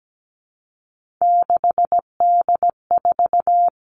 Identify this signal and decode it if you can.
Morse code, heard as a keyed tone, about 17 wpm, 700 Hz. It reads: 6D4
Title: Three Challenging Letter-Number Combinations 17wpm